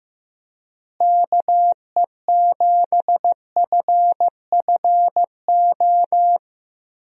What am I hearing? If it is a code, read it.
Morse code, heard as a keyed tone, about 15 wpm, 700 Hz. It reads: KE7FFO
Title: USA Call Signs - 6 Characters - Repeated 3x - 15wpm